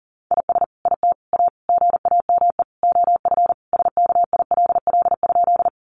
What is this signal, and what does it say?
Morse code, heard as a keyed tone, about 40 wpm, 700 Hz. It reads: IS IT A GAME OF SKILL?
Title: Sentences from Top 1200 Words 40wpm